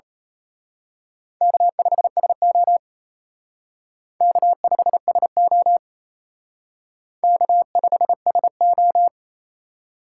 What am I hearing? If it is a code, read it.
Morse code, heard as a keyed tone, about 32 wpm, 700 Hz. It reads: K5SO K5SO K5SO
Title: Speed Racing - USA 1x2 Call Signs 28wpm